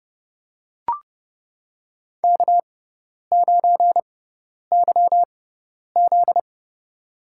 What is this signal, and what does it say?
Morse code, heard as a keyed tone, about 29 wpm, spaced as for 10 wpm, 700 Hz. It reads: K9YZ
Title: USA Call Signs - 4 Characters - 6x Character Spacing - 30wpm